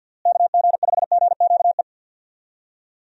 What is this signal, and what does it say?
Morse code, heard as a keyed tone, about 52 wpm, 700 Hz. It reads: KG5GQE
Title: USA Call Signs - 6 Characters - Repeated 3x - 50wpm